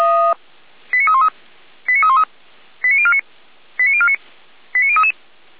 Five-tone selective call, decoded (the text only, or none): 89313 89313 89049 89049 89033